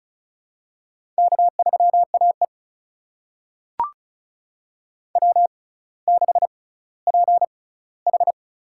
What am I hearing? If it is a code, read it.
Morse code, heard as a keyed tone, about 36 wpm, spaced as for 18 wpm, 700 Hz. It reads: K3AE W6PH
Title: USA Call Signs - 4 Characters - 6x Character Spacing - 35wpm